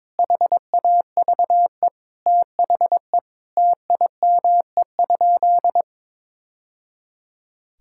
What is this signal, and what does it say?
Morse code, heard as a keyed tone, about 22 wpm, 700 Hz. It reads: HAVE THE TIME?
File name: practice / 22wpm / Sentences with 2-Word Phrases - 22wpm